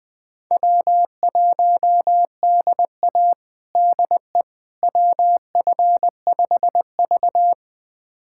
Morse code, heard as a keyed tone, about 20 wpm, 700 Hz. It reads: W1DA DE WF5V
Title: USA Callsign DE Callsign - 4 Characters - 20wpm